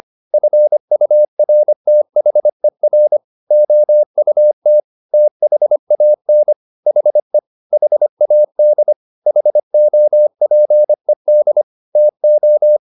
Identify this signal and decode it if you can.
Morse code, heard as a keyed tone, about 25 wpm, 600 Hz. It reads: FURTHER OUT THAN HE HAD HOPED TO
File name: the old man and the sea part 1 25wpm